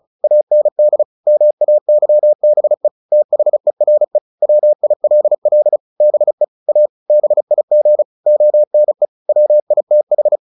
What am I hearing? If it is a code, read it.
Morse code, heard as a keyed tone, about 34 wpm, 600 Hz. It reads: AND MAYBE THERE WILL BE A BIG ONE WITH
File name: the old man and the sea / the old man and the sea 1 35wpm